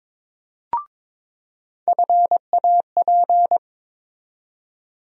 Morse code, heard as a keyed tone, about 22 wpm, 700 Hz. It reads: FAP
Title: USA Call Sign Postfixes - 3 Letters 22wpm